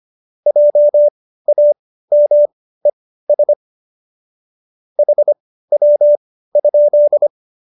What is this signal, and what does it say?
Morse code, heard as a keyed tone, about 25 wpm, spaced as for 15 wpm, 590 Hz. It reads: JAMES HW?